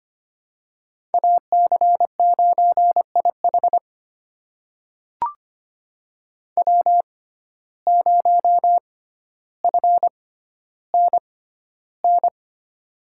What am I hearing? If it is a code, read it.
Morse code, heard as a keyed tone, about 25 wpm, spaced as for 13 wpm, 700 Hz. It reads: AC9IH W0FNN
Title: USA Call Signs - 5 Characters - 6x Character Spacing - 25wpm